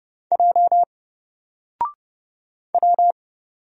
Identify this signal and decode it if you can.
Morse code, heard as a keyed tone, about 30 wpm, 700 Hz. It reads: J W
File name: Single-Letter-Number-30wpm